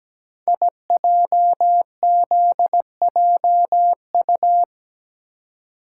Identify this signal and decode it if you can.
Morse code, heard as a keyed tone, about 17 wpm, 700 Hz. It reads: IJZJU
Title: Code Groups - 5 Letter:Numbers - 17wpm